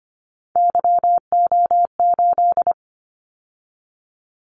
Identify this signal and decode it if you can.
Morse code, heard as a keyed tone, about 25 wpm, 700 Hz. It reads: YO8
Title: Callsign Prefixes for Top 25 DXCC - 25wpm